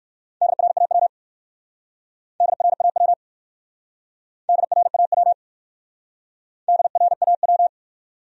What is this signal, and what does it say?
Morse code, heard as a keyed tone, about 60 wpm, 700 Hz. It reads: DRAW DRAW DRAW DRAW